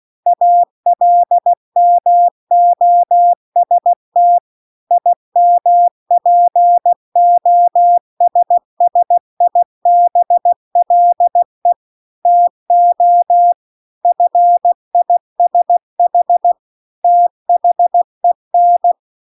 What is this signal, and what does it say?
Morse code, heard as a keyed tone, about 16 wpm, 700 Hz. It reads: ALMOST IMPOSSIBLE TO FISH THEN